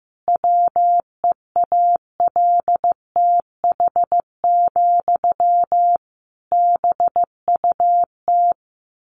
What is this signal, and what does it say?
Morse code, heard as a keyed tone, about 15 wpm, 700 Hz. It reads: WEALTH, BUT
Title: Sentences from Top 1200 Words 15wpm